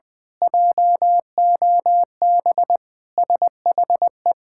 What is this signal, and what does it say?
Morse code, heard as a keyed tone, about 20 wpm, 700 Hz. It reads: JOB SHE
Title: Sentences with 6-Word Phrases - Spoken Beginning - 20wpm